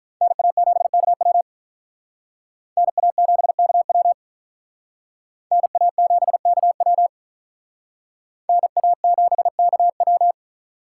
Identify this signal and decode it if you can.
Morse code, heard as a keyed tone, about 48 wpm, 700 Hz. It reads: NA7KW NA7KW NA7KW NA7KW